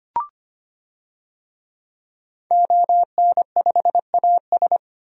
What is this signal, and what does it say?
Morse code, heard as a keyed tone, about 25 wpm, 700 Hz. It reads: ON5AS
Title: DXCC-100 Call Signs - 5 Characters - 25wpm